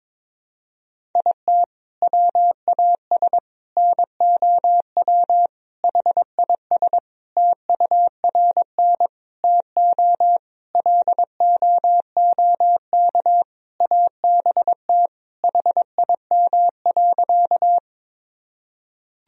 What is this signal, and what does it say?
Morse code, heard as a keyed tone, about 22 wpm, 700 Hz. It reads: IT WAS NOW HIS TURN TO LOOK ABT HIM.
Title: Sentences from Top 200 Words and Abbreviations 22wpm